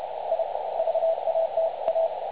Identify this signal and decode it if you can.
Morse code, heard as a keyed tone, about 30 wpm, 650 Hz, with radio noise. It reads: I VATA
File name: dlazaawansowanych / nagraniazpasma / websdr recording 3565.0kHz